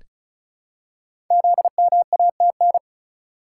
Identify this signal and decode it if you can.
Morse code, heard as a keyed tone, about 36 wpm, 700 Hz. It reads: ZMATN
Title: Code Groups - 5 Letters - 35wpm